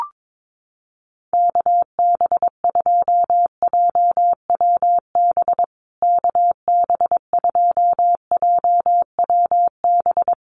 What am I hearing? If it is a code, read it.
Morse code, heard as a keyed tone, about 22 wpm, 700 Hz. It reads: KB2JWB KB2JWB
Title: USA Call Signs Repeated 3x 22wpm